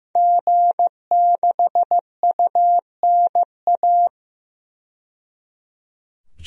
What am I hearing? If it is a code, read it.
Morse code, heard as a keyed tone, about 15 wpm, 700 Hz. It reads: G6UNA
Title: DXCC-100 Call Signs - 5 Characters - 15wpm